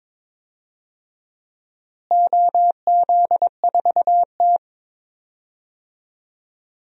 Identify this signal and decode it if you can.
Morse code, heard as a keyed tone, about 22 wpm, 700 Hz. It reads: OZ4T
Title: DXCC-100 Call Signs - 4 Characters - Repeated 2x - 22wpm